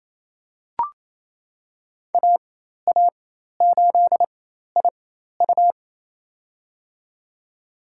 Morse code, heard as a keyed tone, about 27 wpm, spaced as for 13 wpm, 700 Hz. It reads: AA8IU
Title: USA Call Signs - 5 Characters - 4x Character Spacing - 28wpm